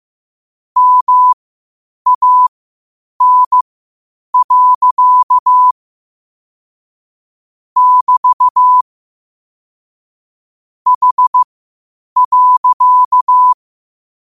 Morse code, heard as a keyed tone, about 15 wpm, spaced as for 8 wpm, 1000 Hz. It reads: MAN. <BT> H.
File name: qotdfiles / 8wpm